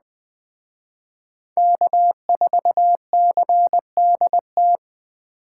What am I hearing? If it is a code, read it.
Morse code, heard as a keyed tone, about 20 wpm, 700 Hz. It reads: K4CDT
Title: Speed Racing - USA 1x3 Call Signs 17wpm